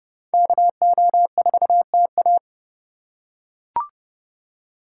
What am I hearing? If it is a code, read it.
Morse code, heard as a keyed tone, about 29 wpm, 700 Hz. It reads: KO4TA